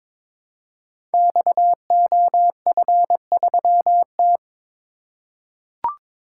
Morse code, heard as a keyed tone, about 22 wpm, 700 Hz. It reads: XOF3T
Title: Code Groups - 5 Letter:Numbers - 22wpm